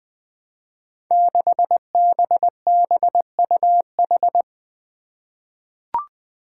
Morse code, heard as a keyed tone, about 21 wpm, 700 Hz. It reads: 6BBUH
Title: Five Challenging Letter-Number Combinations 20wpm